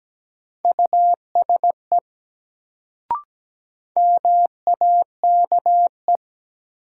Morse code, heard as a keyed tone, about 17 wpm, 700 Hz. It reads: USE MAKE
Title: Sentences from Top 100 Words - New Word Drill - 17wpm